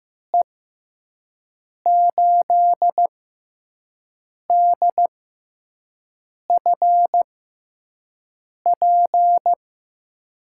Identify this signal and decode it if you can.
Morse code, heard as a keyed tone, about 15 wpm, spaced as for 5 wpm, 700 Hz. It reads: E8DFP
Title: USA Call Signs - 6 Characters - 6x Character Spacing - 15wpm